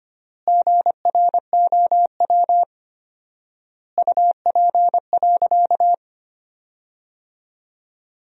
Morse code, heard as a keyed tone, about 25 wpm, 700 Hz. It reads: GROW UP.
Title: 4x Word-Spacing - Sentences from Top 300 Words - 25wpm